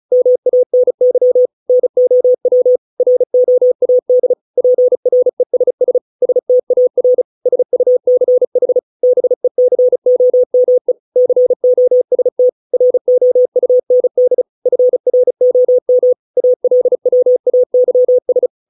Morse code, heard as a keyed tone, about 34 wpm, 500 Hz. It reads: MANY NOW ROAD PRESS STAR SUCH BECOME COST ROUND FROM ALWAYS